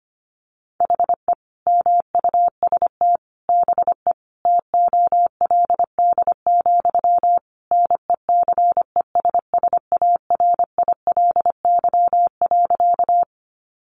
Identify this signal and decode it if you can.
Morse code, heard as a keyed tone, about 25 wpm, 700 Hz. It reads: HE MUST BE TOLD, NECESSARILY.
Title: Sentences from Top 1400 Words 25wpm